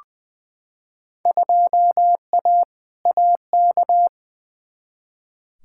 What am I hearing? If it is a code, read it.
Morse code, heard as a keyed tone, about 21 wpm, 700 Hz. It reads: 2A AK
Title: Field Day Exchanges Encore - 20wpm